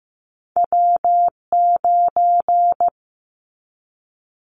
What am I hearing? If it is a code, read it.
Morse code, heard as a keyed tone, about 15 wpm, 700 Hz. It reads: W9
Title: Callsign Prefixes for Top 25 DXCC - 15wpm